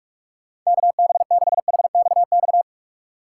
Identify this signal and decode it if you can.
Morse code, heard as a keyed tone, about 46 wpm, 700 Hz. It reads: KB6HXX
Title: USA Call Signs - 6 Characters - 2x Character Spacing - 45wpm